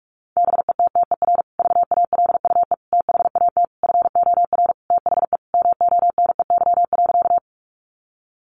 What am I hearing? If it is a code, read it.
Morse code, heard as a keyed tone, about 44 wpm, 700 Hz. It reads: BETTER VALUE THAT FOR THE MONEY.
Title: Sentences from Top 600 Words 45wpm